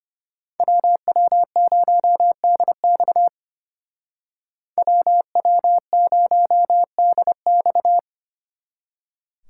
Morse code, heard as a keyed tone, about 31 wpm, 700 Hz. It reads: WW0DX WW0DX